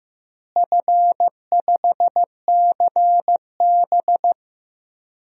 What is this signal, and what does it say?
Morse code, heard as a keyed tone, about 16 wpm, 700 Hz. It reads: F5CB
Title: DXCC-100 Call Signs - 4 Characters - Repeated 2x - 15wpm